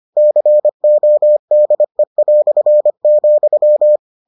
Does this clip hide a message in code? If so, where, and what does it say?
Yes: Morse code from 0.2 to 4.0 s, heard as a keyed tone, about 25 wpm, 600 Hz. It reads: CODE",